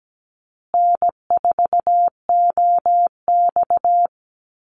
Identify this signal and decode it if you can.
Morse code, heard as a keyed tone, about 17 wpm, 700 Hz. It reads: N4OX
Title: Easy Call Signs Repeated 3x - 17wpm